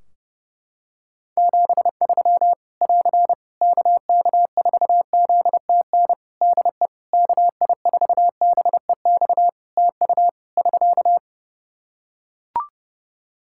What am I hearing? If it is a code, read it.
Morse code, heard as a keyed tone, about 30 wpm, 700 Hz. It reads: 73 <AR> KK4ZTN DE KI4BEX TU <SK>
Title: K1USN SST QSO Practice 30wpm